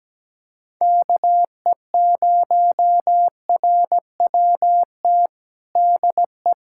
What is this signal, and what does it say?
Morse code, heard as a keyed tone, about 17 wpm, 700 Hz. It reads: KE0RWT DE
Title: USA Callsign DE Callsign - 6 Characters - 17wpm